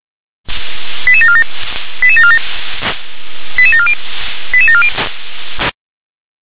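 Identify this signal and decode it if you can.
Five-tone selective call (ZVEI), heard as quick runs of five tones, tones of about 70 ms. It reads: 88747 88747 88740 88740
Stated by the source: ZVEI five-tone call 88913 received on an SDR.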